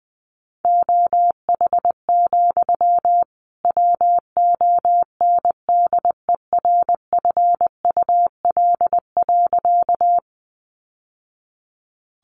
Morse code, heard as a keyed tone, about 20 wpm, 700 Hz. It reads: OH, WONDERFUL.
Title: Sentences from Top 1100 Words 20wpm